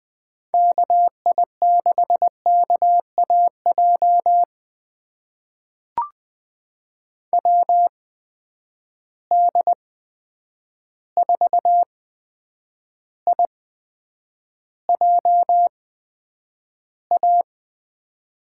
Morse code, heard as a keyed tone, about 20 wpm, spaced as for 9 wpm, 700 Hz. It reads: KI6KAJ WD4IJA
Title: USA Call Signs - 6 Characters - 8x Character Spacing - 20wpm